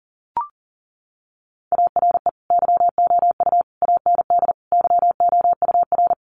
Morse code, heard as a keyed tone, about 40 wpm, 700 Hz. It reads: ARE YOU AND YOUR
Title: Sentences from Top 1400 Words 40wpm